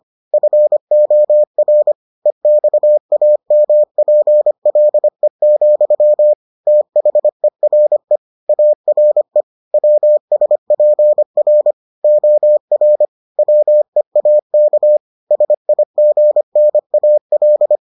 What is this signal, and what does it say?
Morse code, heard as a keyed tone, about 25 wpm, 600 Hz. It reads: FOR EXAMPLE, THERE ARE WSPR OR WEAK SIGNAL